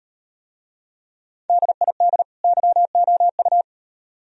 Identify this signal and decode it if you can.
Morse code, heard as a keyed tone, about 38 wpm, 700 Hz. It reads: DID YOU